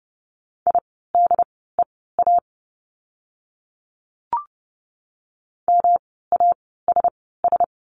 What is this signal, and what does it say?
Morse code, heard as a keyed tone, about 30 wpm, spaced as for 17 wpm, 700 Hz. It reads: IDEA MASS
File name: course / 30:10wpm / 108 Lesson 18f - Letter m - Words 30wpm